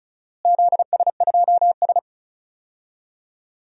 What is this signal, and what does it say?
Morse code, heard as a keyed tone, about 34 wpm, 700 Hz. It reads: ZS2S
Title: DXCC-100 Call Signs - 4 Characters - Repeated 2x - 35wpm